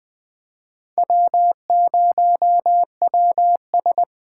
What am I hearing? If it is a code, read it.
Morse code, heard as a keyed tone, about 21 wpm, 700 Hz. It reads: W0WS
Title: Speed Racing - USA 1x2 Call Signs 17wpm